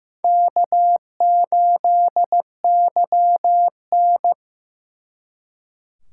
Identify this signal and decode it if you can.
Morse code, heard as a keyed tone, about 18 wpm, 700 Hz. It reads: K8YN
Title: Speed Racing - USA Call Signs 15wpm